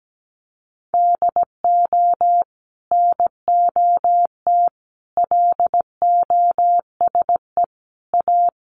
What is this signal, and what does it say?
Morse code, heard as a keyed tone, about 17 wpm, 700 Hz. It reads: DO NOT LOSE A
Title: Sentences from Top 1100 Words 17wpm